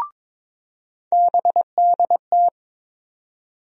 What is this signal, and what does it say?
Morse code, heard as a keyed tone, about 22 wpm, 700 Hz. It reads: BDT
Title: USA Call Sign Postfixes - 3 Letters 22wpm